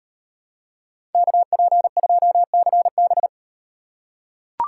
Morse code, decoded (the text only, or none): KP2CB